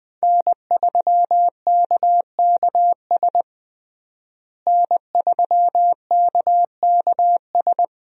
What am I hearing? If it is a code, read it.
Morse code, heard as a keyed tone, about 21 wpm, 700 Hz. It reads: N3KKS N3KKS
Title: USA Call Signs - 5 Characters - Repeated 3x - 20wpm